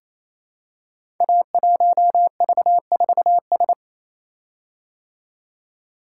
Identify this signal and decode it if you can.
Morse code, heard as a keyed tone, about 27 wpm, 700 Hz. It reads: A1V4S
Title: Code Groups - 5 Letter:Numbers - 28wpm